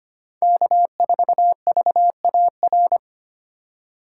Morse code, heard as a keyed tone, about 33 wpm, 700 Hz. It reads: K4VAR